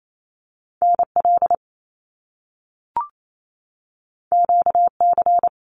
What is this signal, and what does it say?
Morse code, heard as a keyed tone, about 28 wpm, 700 Hz. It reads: NL QC
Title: Canadian Provinces and Territories 28wpm